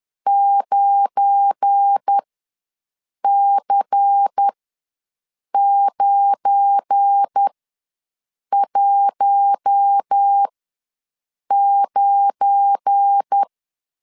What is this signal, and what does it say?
Morse code, heard as a keyed tone, about 10 wpm, spaced as for 6 wpm, 800 Hz. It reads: 9C919
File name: LEZIONI / 7 GF9Z / 7 GF9 ws 10wpm 4034